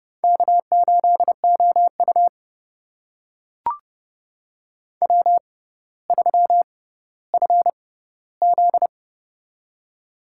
Morse code, heard as a keyed tone, about 30 wpm, spaced as for 15 wpm, 700 Hz. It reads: K8OU W3FZ
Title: USA Call Signs - 4 Characters - 6x Character Spacing - 30wpm